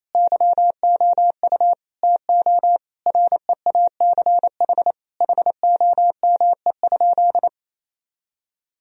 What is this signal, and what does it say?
Morse code, heard as a keyed tone, about 28 wpm, 700 Hz. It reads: YOU TO REACH HOME?
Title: Sentences from Top 1000 Words - Review Part 2 - 28wpm